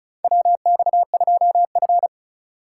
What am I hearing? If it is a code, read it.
Morse code, heard as a keyed tone, about 36 wpm, 700 Hz. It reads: WX2F
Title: USA Call Signs - 4 Characters - 4x Character Spacing - 35wpm